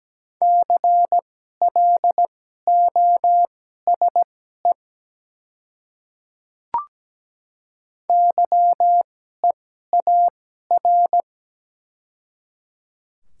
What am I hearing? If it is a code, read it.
Morse code, heard as a keyed tone, about 17 wpm, spaced as for 12 wpm, 700 Hz. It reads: CLOSE YEAR
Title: 2x Character Spacing - Top 300 Words - 17wpm